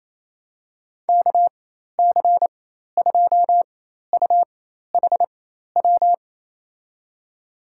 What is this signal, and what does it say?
Morse code, heard as a keyed tone, about 28 wpm, spaced as for 13 wpm, 700 Hz. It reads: KC2UHW
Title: USA Call Signs - 6 Characters - 4x Character Spacing - 28wpm